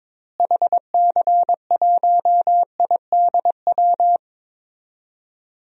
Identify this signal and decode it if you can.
Morse code, heard as a keyed tone, about 22 wpm, 700 Hz. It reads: HC1IDW
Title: DXCC-100 Call Signs - 6 Characters - Repeated 3x - 22wpm